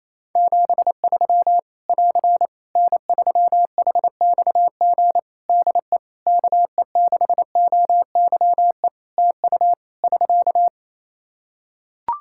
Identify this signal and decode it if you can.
Morse code, heard as a keyed tone, about 28 wpm, 700 Hz. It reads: 73 <AR> N3HXG DE KE6OYE TU <SK>
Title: Practice QSO - Exchange 4 - Ending - 28wpm